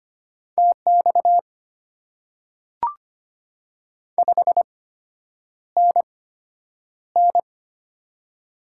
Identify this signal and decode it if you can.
Morse code, heard as a keyed tone, about 25 wpm, spaced as for 9 wpm, 690 Hz. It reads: TX 5NN
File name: Top 90 QSO Elements - 8x Character Spacing - 25wpm